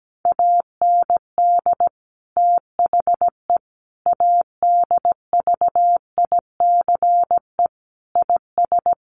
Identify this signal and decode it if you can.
Morse code, heard as a keyed tone, about 17 wpm, 700 Hz. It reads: AND THE ADVICE IS
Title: Sentences from Top 1100 Words 17wpm